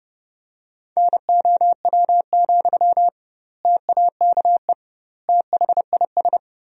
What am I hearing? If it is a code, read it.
Morse code, heard as a keyed tone, about 32 wpm, 700 Hz. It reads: NOW, TAKE THIS
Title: 2x Word-Spacing - Sentences from Top 400 Words - 30wpm